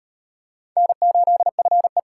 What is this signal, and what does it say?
Morse code, heard as a keyed tone, about 35 wpm, 700 Hz. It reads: N8FE